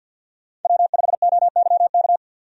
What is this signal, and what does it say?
Morse code, heard as a keyed tone, about 50 wpm, 700 Hz. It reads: W5OYK
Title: USA Call Signs - 5 Characters - 50wpm